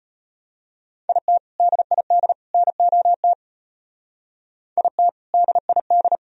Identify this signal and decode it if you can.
Morse code, heard as a keyed tone, about 32 wpm, 700 Hz. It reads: IT DID NOT IT DID